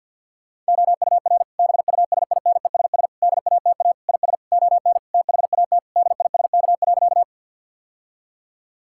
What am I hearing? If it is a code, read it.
Morse code, heard as a keyed tone, about 50 wpm, 700 Hz. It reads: OUR BUSINESS DATA IS ON THAT DISK.
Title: Sentences from Top 1000 Words - Review Part 6 - 50wpm